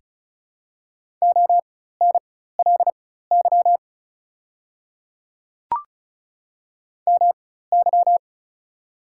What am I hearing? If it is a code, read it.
Morse code, heard as a keyed tone, about 36 wpm, spaced as for 16 wpm, 690 Hz. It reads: ONLY MY
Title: 4x Character Spacing - Top 200 Words - 35wpm